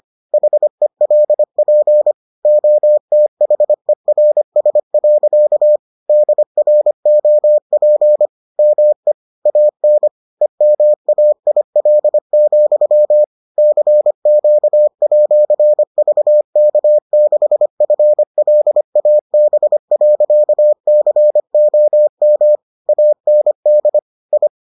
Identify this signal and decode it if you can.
Morse code, heard as a keyed tone, about 25 wpm, 600 Hz. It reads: HELP OTHERS. DROP ME AN EMAIL, CQ@VK6FLAB.COM AND I